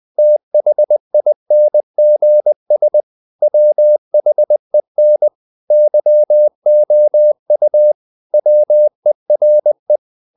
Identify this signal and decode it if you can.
Morse code, heard as a keyed tone, about 20 wpm, 600 Hz. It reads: THINGS WHEN YOU WERE
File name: the old man and the sea / the old man and the sea part 1 20wpm